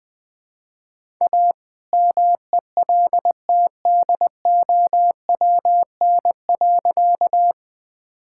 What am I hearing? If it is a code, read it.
Morse code, heard as a keyed tone, about 20 wpm, 700 Hz. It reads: A MELTDOWN.
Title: All-Age Appropriate Jokes - 20wpm